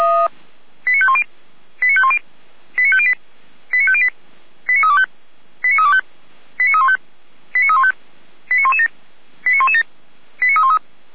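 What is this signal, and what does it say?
Five-tone selective call (ZVEI), heard as quick runs of five tones, tones of about 70 ms. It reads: 89510 89510 89508 89508 89325 89325 89315 89315 89117 89117 89313
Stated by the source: ZVEI five-tone selective call to address 89229, received via SDR.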